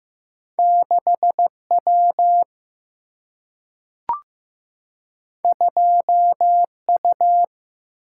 Morse code, heard as a keyed tone, about 15 wpm, 700 Hz. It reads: 6W 2U
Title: Single-Letter Single-Number 15wpm